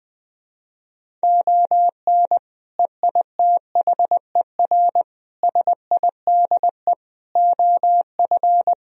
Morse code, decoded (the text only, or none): ON EITHER SIDE OF